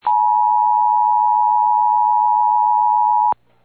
Emergency Alert System (SAME) tones (the 853 and 960 Hz attention tone). It is heard from 0.0 to 3.4 s.
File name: eas / KWBE / KWBE